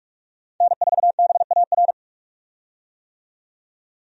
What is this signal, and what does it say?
Morse code, heard as a keyed tone, about 45 wpm, 700 Hz. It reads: N4BAR
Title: USA Call Signs - 5 Characters - 45wpm